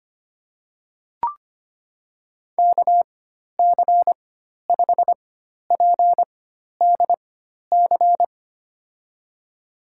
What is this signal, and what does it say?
Morse code, heard as a keyed tone, about 25 wpm, spaced as for 12 wpm, 700 Hz. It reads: KC5PDC
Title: USA Call Signs - 6 Characters - 4x Character Spacing - 25wpm